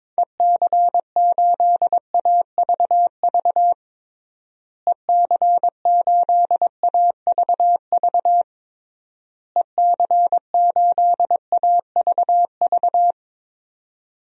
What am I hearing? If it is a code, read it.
Morse code, heard as a keyed tone, about 22 wpm, 700 Hz. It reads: EC8AVV EC8AVV EC8AVV
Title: DXCC-100 Call Signs - 6 Characters - Repeated 3x - 22wpm